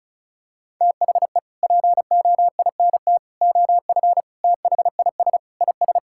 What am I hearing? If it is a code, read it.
Morse code, heard as a keyed tone, about 34 wpm, 700 Hz. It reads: THE POINT OF THIS IS
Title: Sentences from Top 1600 Words 35wpm